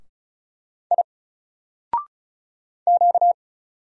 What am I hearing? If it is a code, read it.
Morse code, heard as a keyed tone, about 34 wpm, 700 Hz. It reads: I Q
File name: Single-Letter-Number-35wpm